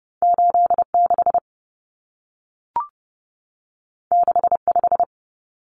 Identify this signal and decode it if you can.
Morse code, heard as a keyed tone, about 30 wpm, 700 Hz. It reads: 86 65
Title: Numbers - 2-Digit 30wpm